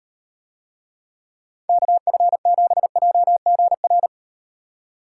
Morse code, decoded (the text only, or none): KF7JGR